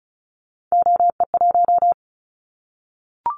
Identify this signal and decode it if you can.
Morse code, heard as a keyed tone, about 36 wpm, 700 Hz. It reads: OE1
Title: Callsign Prefixes for Top 50 DXCC - 35wpm